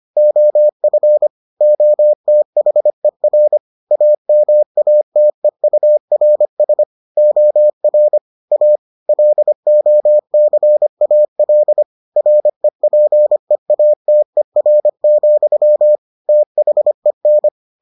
Morse code, heard as a keyed tone, about 25 wpm, 600 Hz. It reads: OF OTHER AMATEURS OR A LOCAL REPEATER, THEN